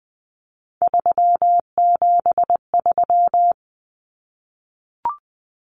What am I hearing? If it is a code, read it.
Morse code, heard as a keyed tone, about 20 wpm, 700 Hz. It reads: 373
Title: Numbers - 3-Digit 20wpm